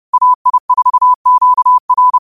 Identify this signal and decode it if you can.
Morse code, heard as a keyed tone, about 30 wpm, 1000 Hz. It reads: AI4QR
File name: qotdfiles / 30wpm